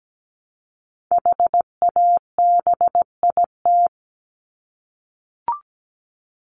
Morse code, heard as a keyed tone, about 17 wpm, 700 Hz. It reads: HABIT
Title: Top 1751-2000 Words 17wpm